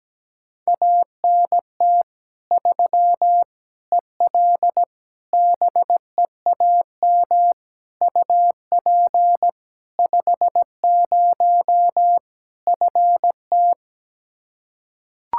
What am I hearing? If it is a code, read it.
Morse code, heard as a keyed tone, about 17 wpm, 700 Hz. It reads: ANT 3 EL BEAM UP 50 FT
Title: Practice QSO - Exchange 2 - Info - 17wpm